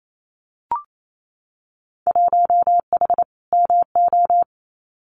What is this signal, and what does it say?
Morse code, heard as a keyed tone, about 28 wpm, 700 Hz. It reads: 1H MO
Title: Winter Field Day Exchanges 28wpm